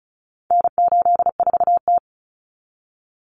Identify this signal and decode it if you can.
Morse code, heard as a keyed tone, about 34 wpm, 700 Hz. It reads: N84T
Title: Common Cut Numbers - 4-Digit 35wpm